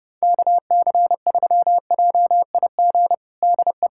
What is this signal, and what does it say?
Morse code, heard as a keyed tone, about 32 wpm, 700 Hz. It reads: KC3JIG DE
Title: USA Callsign DE Callsign 30wpm